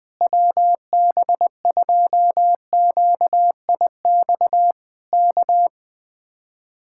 Morse code, heard as a keyed tone, about 20 wpm, 700 Hz. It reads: WB2QIX K
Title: Speed-Racing Common QSO Phrases Encore - 17wpm